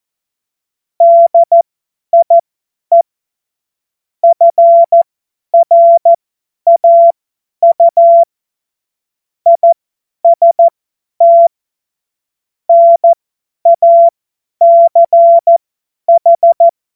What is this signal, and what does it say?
Morse code, heard as a keyed tone, about 14 wpm, spaced as for 10 wpm, 680 Hz. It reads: DIE FRAU IST NACH
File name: Morsekurs / Witze / Witz 03. Januar 2023 50 F70